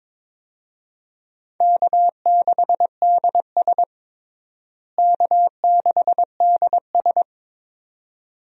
Morse code, heard as a keyed tone, about 22 wpm, 700 Hz. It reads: K6DS K6DS
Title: USA Call Signs - 4 Characters - Repeated 2x - 22wpm